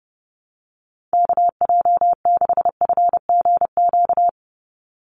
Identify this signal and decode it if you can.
Morse code, heard as a keyed tone, about 30 wpm, 700 Hz. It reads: KJ6FGQ
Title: USA Call Signs 30wpm